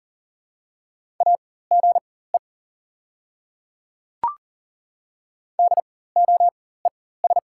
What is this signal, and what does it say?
Morse code, heard as a keyed tone, about 38 wpm, spaced as for 18 wpm, 690 Hz. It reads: AGE DOES
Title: 4x Character Spacing - Top 500 Words - 40wpm